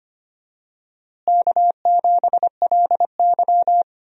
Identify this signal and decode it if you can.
Morse code, heard as a keyed tone, about 25 wpm, 700 Hz. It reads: K7LY
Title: USA Call Signs - 4 Characters - Repeated 3x - 25wpm